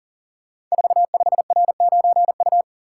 Morse code, heard as a keyed tone, about 40 wpm, 700 Hz. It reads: 45R9U
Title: Code Groups - 5 Letter:Numbers - 40wpm